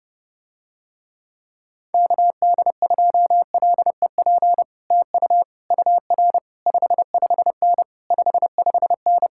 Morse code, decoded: KD2LEP TU UR 55N 55N

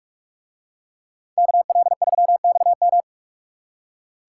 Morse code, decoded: KL3XM

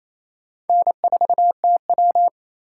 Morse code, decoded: N4TW

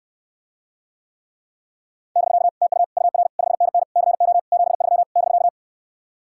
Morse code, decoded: <BK> TU UR 5NN CO 73 <BK>

